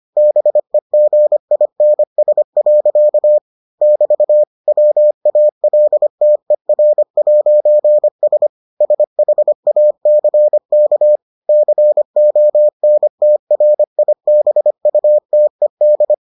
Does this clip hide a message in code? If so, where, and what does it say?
Yes: Morse code from 0.2 to 16.1 s, heard as a keyed tone, about 25 wpm, 600 Hz. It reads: BEGINS. <BT> WALTER'S SHACK CONTRIBUTED